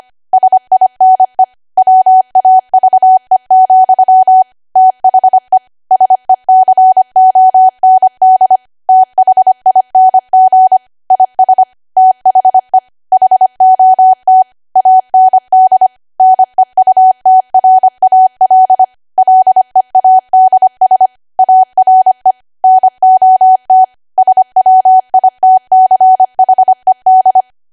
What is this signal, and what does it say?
Morse code, heard as a keyed tone, about 25 wpm, 750 Hz. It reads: SINE WAVE, THE SECOND THING IS THE HOT AND NEUTRAL LEADS ARE NOT SWITCHED
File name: cwa / W1AW-Level3 / 25WPM